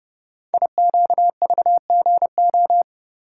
Morse code, heard as a keyed tone, about 30 wpm, 700 Hz. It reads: IQVGO
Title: Code Groups - 5 Letters - 30wpm